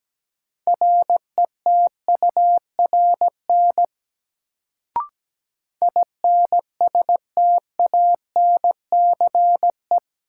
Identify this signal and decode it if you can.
Morse code, heard as a keyed tone, about 17 wpm, 700 Hz. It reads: RETURN INSTANCE
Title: Sentences from Top 800 Words - New Word Drill - 17wpm